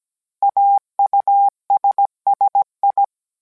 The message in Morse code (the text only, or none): AUSSI